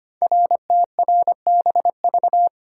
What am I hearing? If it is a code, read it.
Morse code, heard as a keyed tone, about 25 wpm, 700 Hz. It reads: RTRBV
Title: Code Groups - 5 Letters - 25wpm